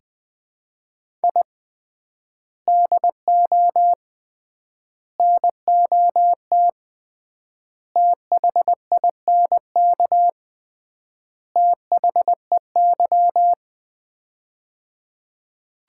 Morse code, heard as a keyed tone, about 20 wpm, 700 Hz. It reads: I DO NOT THINK THEY
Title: 3x Word-Spacing - Sets of 5 Words - 20wpm